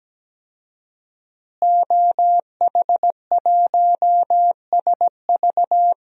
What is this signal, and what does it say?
Morse code, heard as a keyed tone, about 17 wpm, 700 Hz. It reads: OH1SV
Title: DXCC-100 Call Signs - 5 Characters - Repeated 2x - 17wpm